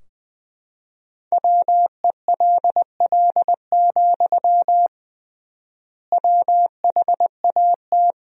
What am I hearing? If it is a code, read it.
Morse code, heard as a keyed tone, about 21 wpm, 700 Hz. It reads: WELL, WHAT
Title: 3x Word-Spacing - Sentences from Top 200 Words - 20wpm